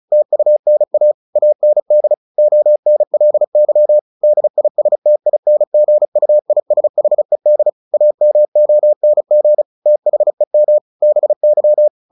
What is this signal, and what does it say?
Morse code, heard as a keyed tone, about 34 wpm, 600 Hz. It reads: TUNA AND ONLY DISTINGUISHED AMONG THEM BY
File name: the old man and the sea / the old man and the sea 1 35wpm